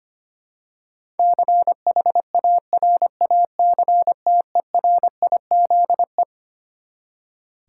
Morse code, heard as a keyed tone, about 25 wpm, 700 Hz. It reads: CHARACTERIZE
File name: Twelve Letter Words - 25wpm